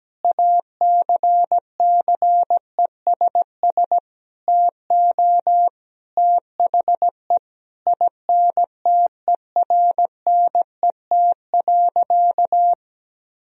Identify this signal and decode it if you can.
Morse code, heard as a keyed tone, about 17 wpm, 700 Hz. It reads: ACCESS TO THE INTERNET.
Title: Sentences from Top 1000 Words - Review Part 4 - 17wpm